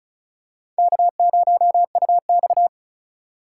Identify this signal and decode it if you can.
Morse code, heard as a keyed tone, about 36 wpm, 700 Hz. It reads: K0UX
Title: USA Call Signs - 3x Character Spacing - 35wpm